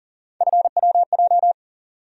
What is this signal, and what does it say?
Morse code, heard as a keyed tone, about 40 wpm, 700 Hz. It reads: FWJ
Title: USA Call Sign Postfixes - 3 Letters 40wpm